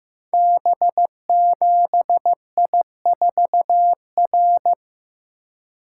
Morse code, heard as a keyed tone, about 15 wpm, 700 Hz. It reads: B7I4R
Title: Code Groups - 5 Letter:Numbers - 15wpm